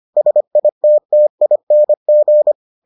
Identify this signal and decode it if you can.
Morse code, heard as a keyed tone, about 25 wpm, 600 Hz. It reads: SITTING